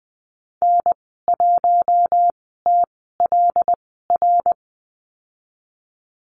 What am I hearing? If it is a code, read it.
Morse code, heard as a keyed tone, about 20 wpm, spaced as for 15 wpm, 700 Hz. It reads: N1TLR
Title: 076 Lesson 13g - Letter L - Call Signs 20wpm